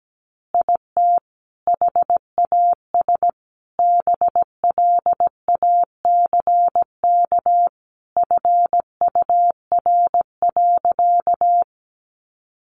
Morse code, heard as a keyed tone, about 17 wpm, 700 Hz. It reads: IT HAS BLACK FUR.
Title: Easy Phrases 17wpm